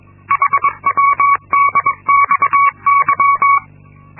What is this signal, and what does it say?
Morse code, heard as a keyed tone, about 22 wpm, 1120 Hz. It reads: HWDXY
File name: Linksmakalnis-2